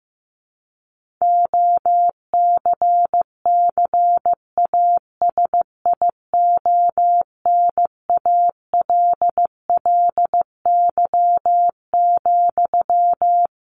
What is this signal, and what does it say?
Morse code, heard as a keyed tone, about 15 wpm, 700 Hz. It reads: OCCASIONALLY,
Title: Sentences from Top 1400 Words 15wpm